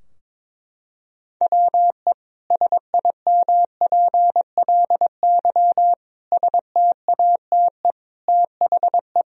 Morse code, heard as a keyed tone, about 22 wpm, 700 Hz. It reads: WE SIMPLY STATE THE
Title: Sentences from Top 1000 Words - Review Part 4 - 22wpm